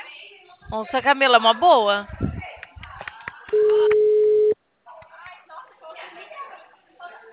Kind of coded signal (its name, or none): telephone call progress tone